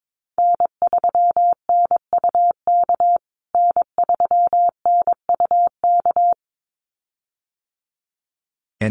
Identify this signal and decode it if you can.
Morse code, heard as a keyed tone, about 22 wpm, 700 Hz. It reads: N3NUK N3NUK
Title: USA Call Signs Repeated 2x 22wpm